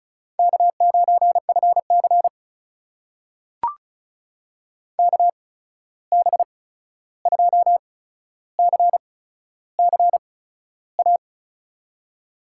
Morse code, heard as a keyed tone, about 36 wpm, spaced as for 13 wpm, 700 Hz. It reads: K9FC KB2CCA
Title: USA Call Signs - 8x Character Spacing - 35wpm